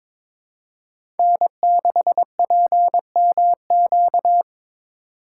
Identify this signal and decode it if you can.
Morse code, heard as a keyed tone, about 22 wpm, 700 Hz. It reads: N6PMQ
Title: USA Call Signs - 5 Characters - 2x Character Spacing - 22wpm